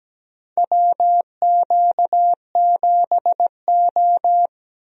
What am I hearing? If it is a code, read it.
Morse code, heard as a keyed tone, about 17 wpm, 700 Hz. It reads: WQ7O
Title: USA Call Signs - 4 Characters - Repeated 2x - 17wpm